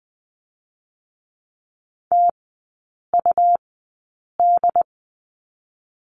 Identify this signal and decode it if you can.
Morse code, heard as a keyed tone, about 20 wpm, spaced as for 8 wpm, 700 Hz. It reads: TUD
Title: 184 Lesson 30e - Letter b - Three Characters 20wpm